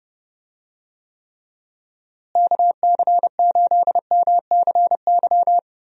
Morse code, heard as a keyed tone, about 30 wpm, 700 Hz. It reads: KC8MCY
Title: USA Call Signs - 6 Characters - Repeated 3x - 30wpm